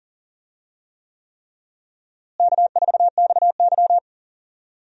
Morse code, heard as a keyed tone, about 37 wpm, 700 Hz. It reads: K4XY
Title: USA Call Signs - 4 Characters - Repeated 3x - 40wpm